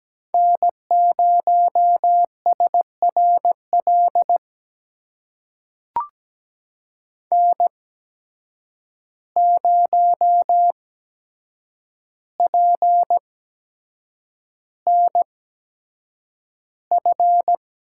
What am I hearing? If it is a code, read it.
Morse code, heard as a keyed tone, about 17 wpm, spaced as for 7 wpm, 700 Hz. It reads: N0SRL N0PNF
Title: USA Call Signs - 5 Characters - 8x Character Spacing - 17wpm